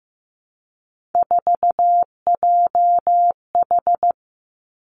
Morse code, heard as a keyed tone, about 15 wpm, 700 Hz. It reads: 4JH
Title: Three Challenging Letter-Number Combinations 15wpm